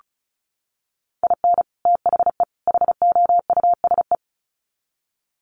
Morse code, heard as a keyed tone, about 34 wpm, 700 Hz. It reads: IN THE HOUSE
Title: Sets of 3 Words 35wpm